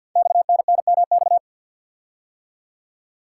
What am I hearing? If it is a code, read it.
Morse code, heard as a keyed tone, about 52 wpm, 700 Hz. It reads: XNNMX
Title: Code Groups - 5 Letters - 50wpm